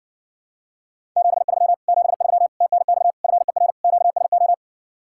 Morse code, heard as a keyed tone, about 60 wpm, 700 Hz. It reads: 73 73 TNX FER QSO